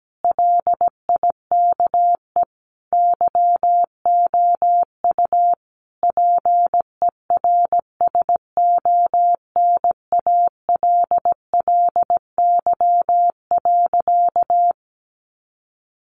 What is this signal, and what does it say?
Morse code, heard as a keyed tone, about 17 wpm, 700 Hz. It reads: LIKE YOU PERSONALLY.
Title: Sentences from Top 1400 Words 17wpm